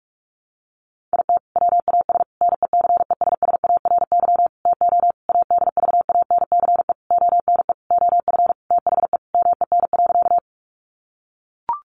Morse code, heard as a keyed tone, about 44 wpm, 700 Hz. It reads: IT WAS NECESSARY TO ADVANCE ONE OF THE MEN.
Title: Sentences from Top 1400 Words 45wpm